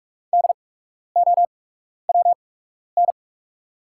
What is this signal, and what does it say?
Morse code, heard as a keyed tone, about 44 wpm, spaced as for 12 wpm, 690 Hz. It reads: DOWN
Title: 8x Character Spacing - Top 300 Words - 45wpm